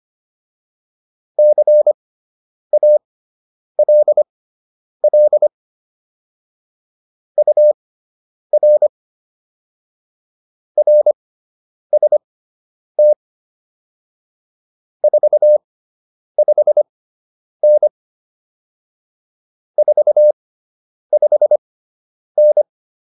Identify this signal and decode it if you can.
Morse code, heard as a keyed tone, about 25 wpm, spaced as for 9 wpm, 600 Hz. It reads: CALL UR RST 45N 45N